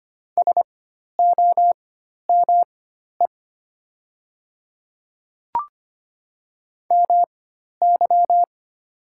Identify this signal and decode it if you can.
Morse code, heard as a keyed tone, about 25 wpm, spaced as for 12 wpm, 690 Hz. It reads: SOME MY
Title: 4x Character Spacing - Top 100 Words - 25wpm